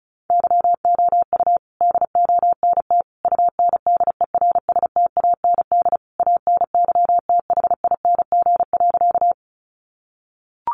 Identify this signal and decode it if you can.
Morse code, heard as a keyed tone, about 34 wpm, 700 Hz. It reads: YOU DONT UNDERSTAND ANYTHING.
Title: Sentences from Top 1000 Words 35wpm